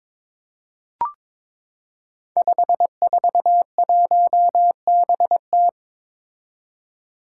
Morse code, heard as a keyed tone, about 22 wpm, 700 Hz. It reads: 541BT
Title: Five Challenging Letter-Number Combinations 22wpm